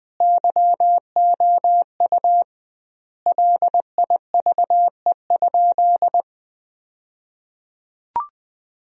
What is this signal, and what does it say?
Morse code, heard as a keyed tone, about 21 wpm, 700 Hz. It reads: YOU LIVE?
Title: 2x Word-Spacing - Sentences from Top 200 Words - 20wpm